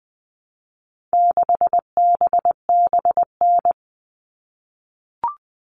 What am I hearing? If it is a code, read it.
Morse code, heard as a keyed tone, about 20 wpm, 700 Hz. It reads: 6BBN
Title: All Cut Numbers - 4-Digit 20wpm